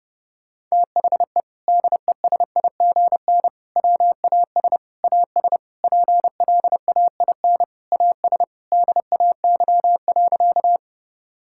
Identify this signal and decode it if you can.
Morse code, heard as a keyed tone, about 30 wpm, 700 Hz. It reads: THE DESIGN WAS AS PLAIN AS DAY.
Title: Sentences from Top 1000 Words - Review Part 5 - 30wpm